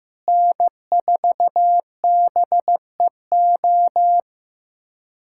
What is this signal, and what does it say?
Morse code, heard as a keyed tone, about 15 wpm, 700 Hz. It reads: N4BEO
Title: USA Callsign DE Callsign 15wpm